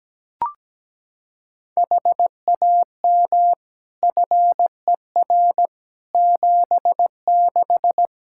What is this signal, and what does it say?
Morse code, heard as a keyed tone, about 17 wpm, 700 Hz. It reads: HAM FER 76
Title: Practice QSO - Exchange 3 - More Info - 17wpm